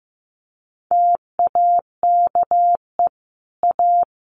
Morse code, heard as a keyed tone, about 15 wpm, 700 Hz. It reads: TAKE A
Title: Sets of 5 Words 15wpm